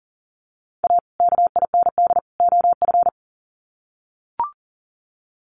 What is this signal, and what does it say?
Morse code, heard as a keyed tone, about 40 wpm, 700 Hz. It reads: A KIND OF
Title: Sets of 3 Words 40wpm